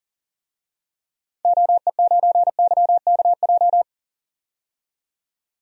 Morse code, heard as a keyed tone, about 40 wpm, 700 Hz. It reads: OE9YKJ
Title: DXCC-100 Call Signs - 6 Characters - 40wpm